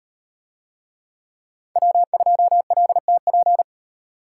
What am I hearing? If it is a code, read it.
Morse code, heard as a keyed tone, about 38 wpm, 700 Hz. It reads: W2LTP